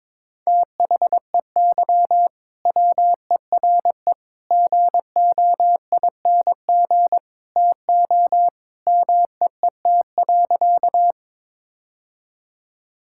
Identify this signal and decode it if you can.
Morse code, heard as a keyed tone, about 21 wpm, 700 Hz. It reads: THEY WERE GOING TO MEET.
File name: practice / 22wpm / Sentences with 3-Word Phrases - 22wpm